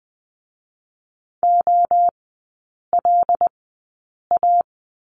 Morse code, heard as a keyed tone, about 21 wpm, spaced as for 8 wpm, 700 Hz. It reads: OLA
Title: 107 Lesson 18e - Letter m - Three Characters 20wpm